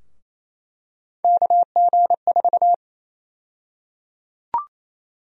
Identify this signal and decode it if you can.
Morse code, heard as a keyed tone, about 28 wpm, 700 Hz. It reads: KG4
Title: USA Call Sign Prefixes - 28wpm